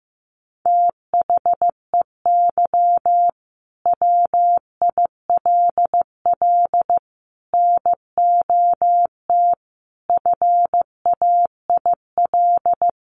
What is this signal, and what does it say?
Morse code, heard as a keyed tone, about 15 wpm, 700 Hz. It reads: THEY WILL NOT FAIL
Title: Sentences from Top 1100 Words 15wpm